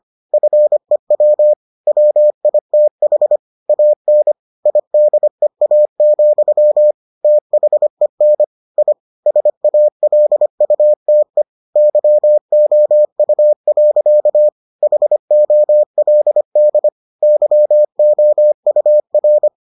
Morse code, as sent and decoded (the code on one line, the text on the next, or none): ..-. .-- / .-- .. - .... / .- -. / .. -.. .- --..-- / - .... -. / .. / ... .- .-.. ..- - . / -.-- --- ..- .-.-.- / .... --- .-.. -.. / -.-- --- ..- .-.
FEW WITH AN IDEA, THEN I SALUTE YOU. HOLD YOUR